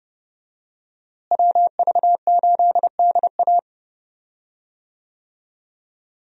Morse code, heard as a keyed tone, about 30 wpm, 700 Hz. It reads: WV8DA
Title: USA Call Signs - 5 Characters - Repeated 3x - 30wpm